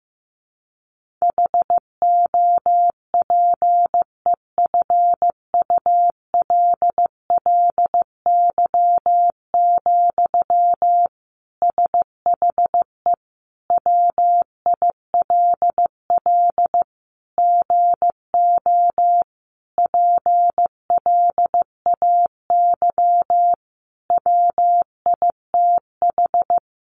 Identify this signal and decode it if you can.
Morse code, heard as a keyed tone, about 15 wpm, 700 Hz. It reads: HOPEFULLY, SHE WILL GO PLAY WITH
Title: Sentences from Top 1500 Words 15wpm